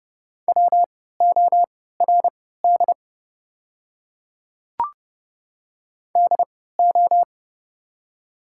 Morse code, heard as a keyed tone, about 30 wpm, spaced as for 17 wpm, 700 Hz. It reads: WORD DO
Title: 3x Character Spacing - Top 100 Words - 30wpm